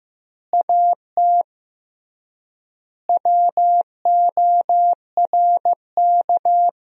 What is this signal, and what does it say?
Morse code, heard as a keyed tone, about 15 wpm, 700 Hz. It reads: AT WORK
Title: 3x Word-Spacing - Sentences from Top 200 Words - 15wpm